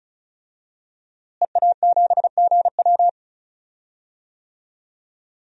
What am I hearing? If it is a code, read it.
Morse code, heard as a keyed tone, about 36 wpm, 700 Hz. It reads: EA7GW
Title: DXCC-100 Call Signs - 5 Characters - Repeated 2x - 35wpm